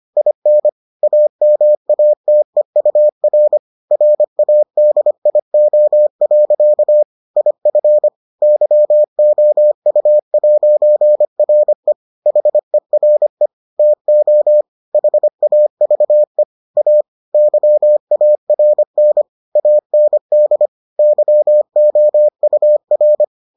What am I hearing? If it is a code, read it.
Morse code, heard as a keyed tone, about 25 wpm, 600 Hz. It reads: IN AMATEUR RADIO. IF YOU'RE HERE TO HAVE A YARN AND YOUR